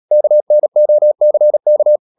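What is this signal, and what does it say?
Morse code, heard as a keyed tone, about 37 wpm, 600 Hz. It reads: KNOCK
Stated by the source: Morse code practice at 35–39 wpm sending English text of Ken Follett's novel Never, from CW med Gnister.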